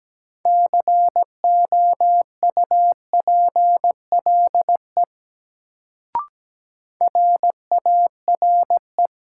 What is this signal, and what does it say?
Morse code, heard as a keyed tone, about 17 wpm, 700 Hz. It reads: COUPLE RARE
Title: Sentences from Top 1100 Words - New Word Drill Alt - 17wpm